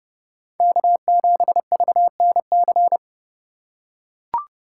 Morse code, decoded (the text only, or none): K7VNC